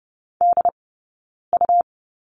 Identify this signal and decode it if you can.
Morse code, heard as a keyed tone, about 30 wpm, spaced as for 9 wpm, 690 Hz. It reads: DU
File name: Lesson 17d - Letter u - Two Characters 30wpm